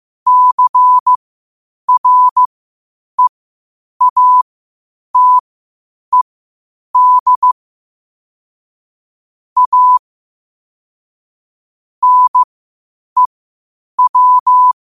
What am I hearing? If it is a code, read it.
Morse code, heard as a keyed tone, about 15 wpm, spaced as for 8 wpm, 1000 Hz. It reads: CREATED A NEW
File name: qotdfiles / 8wpm